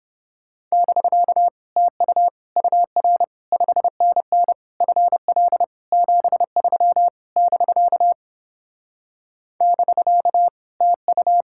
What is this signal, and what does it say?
Morse code, heard as a keyed tone, about 31 wpm, 700 Hz. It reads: <BK> TU UR 5NN FL 73 <BK> <BK> TU